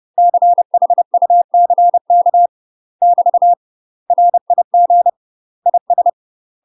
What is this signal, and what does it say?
Morse code, heard as a keyed tone, about 30 wpm, 700 Hz. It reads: CHUCK <BT> RIG IS
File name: UK MASTER QSOs TRAINING / 30WPM / UK MASTER QSOs 0040